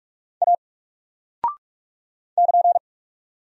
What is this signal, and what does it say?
Morse code, heard as a keyed tone, about 44 wpm, 700 Hz. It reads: A <KN>
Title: Single Character (Letter:Number:Punctuation:Prosign) 45wpm